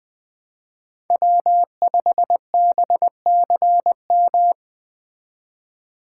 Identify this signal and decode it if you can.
Morse code, heard as a keyed tone, about 21 wpm, 700 Hz. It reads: W5BCM